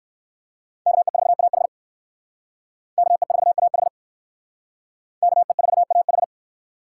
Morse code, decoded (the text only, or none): KE4AH KE4AH KE4AH